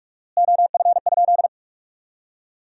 Morse code, heard as a keyed tone, about 42 wpm, 700 Hz. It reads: OF?
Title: 4x Word-Spacing - Sentences from Top 200 Words - 45wpm